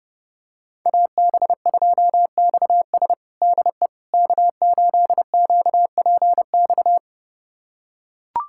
Morse code, heard as a keyed tone, about 30 wpm, 700 Hz. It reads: AB2XS DE K8QPX
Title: USA Callsign DE Callsign - 5 Characters - 30wpm